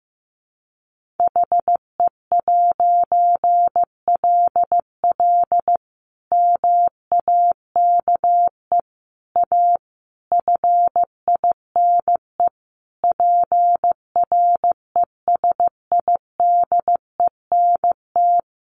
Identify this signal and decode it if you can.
Morse code, heard as a keyed tone, about 15 wpm, 700 Hz. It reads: HE'LL MAKE A FINE PRESIDENT
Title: A Sound Of Thunder 15wpm